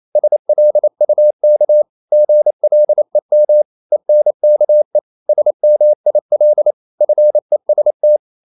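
Morse code, heard as a keyed tone, about 28 wpm, 600 Hz. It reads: SLUK GLEM ENKE SMIL FEST
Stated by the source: Danish short words sent in Morse at 28 wpm, from CW med Gnister.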